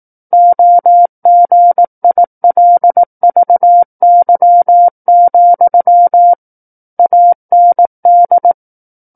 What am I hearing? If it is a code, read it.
Morse code, heard as a keyed tone, about 18 wpm, 700 Hz. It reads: OGILVY, AND
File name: 18 WPM